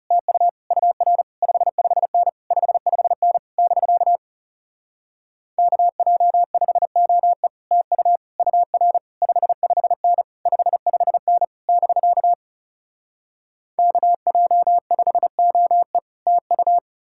Morse code, heard as a keyed tone, about 37 wpm, 700 Hz. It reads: TU UR 55N 55N <BK> KJ5OE TU UR 55N 55N <BK> KJ5OE TU
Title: Speed-Racing Common QSO Phrases - 30wpm